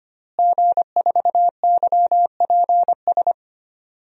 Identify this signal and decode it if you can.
Morse code, heard as a keyed tone, about 25 wpm, 700 Hz. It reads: G4YPS
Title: DXCC-100 Call Signs - 5 Characters - Repeated 2x - 25wpm